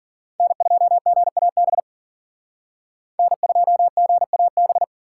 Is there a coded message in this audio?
Yes: Morse code from 0.4 to 4.8 s, heard as a keyed tone, about 50 wpm, 700 Hz. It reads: N2GAB N2GAB